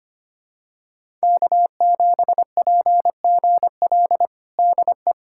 Morse code, heard as a keyed tone, about 25 wpm, 700 Hz. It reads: K7PGL DE
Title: USA Callsign DE Callsign - 5 Characters - 25wpm